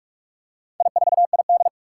Morse code, heard as a keyed tone, about 44 wpm, 700 Hz. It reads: I4ID
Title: Four Challenging Letter-Number Combinations 45wpm